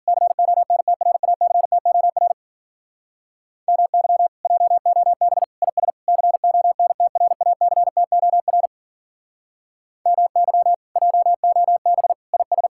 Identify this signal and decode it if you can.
Morse code, heard as a keyed tone, about 48 wpm, 700 Hz. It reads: CONTRACTOR MY JOB IS CONTRACTOR MY JOB IS
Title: Speed-Racing Common QSO Phrases - 40wpm